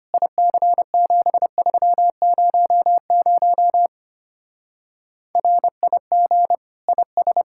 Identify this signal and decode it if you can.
Morse code, heard as a keyed tone, about 31 wpm, 700 Hz. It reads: IC7300 RIG IS